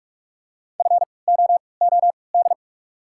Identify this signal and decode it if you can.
Morse code, heard as a keyed tone, about 48 wpm, 700 Hz. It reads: F O O D
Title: 3x Character Spacing - Top 300 Words - 45wpm